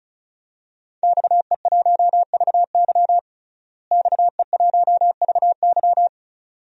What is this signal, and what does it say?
Morse code, heard as a keyed tone, about 36 wpm, 700 Hz. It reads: XE1VY XE1VY
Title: DXCC-100 Call Signs - 5 Characters - Repeated 2x - 35wpm